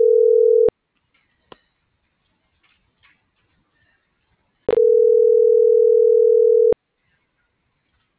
Ringback tone (North America), 440 + 480 Hz.